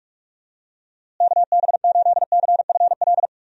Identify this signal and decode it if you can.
Morse code, heard as a keyed tone, about 46 wpm, 700 Hz. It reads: KB8CFL
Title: USA Call Signs - 6 Characters - Repeated 2x - 45wpm